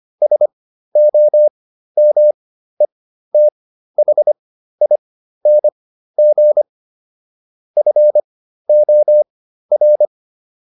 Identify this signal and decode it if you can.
Morse code, heard as a keyed tone, about 25 wpm, spaced as for 13 wpm, 600 Hz. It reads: SOMETHING FOR